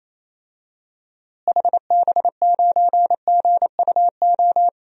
Morse code, heard as a keyed tone, about 28 wpm, 700 Hz. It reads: HB9GUO
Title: DXCC-100 Call Signs - 6 Characters - Repeated 3x - 28wpm